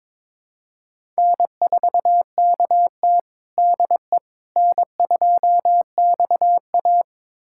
Morse code, heard as a keyed tone, about 22 wpm, 700 Hz. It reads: N4KT DE N2XA
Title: USA Callsign DE Callsign - 4 Characters - 22wpm